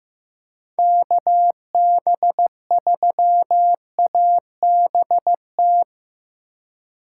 Morse code, heard as a keyed tone, about 15 wpm, 700 Hz. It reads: KB3ABT
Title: USA Call Signs - 6 Characters - 3x Character Spacing - 15wpm